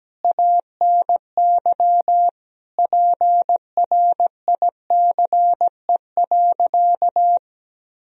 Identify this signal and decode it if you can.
Morse code, heard as a keyed tone, about 17 wpm, 700 Hz. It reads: ANY PRICE.
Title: Sentences from Top 1000 Words - Review Part 4 - 17wpm